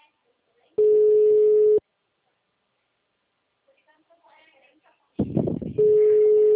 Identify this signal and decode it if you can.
Special information tone, with speech.
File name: out- -2025- - - .15260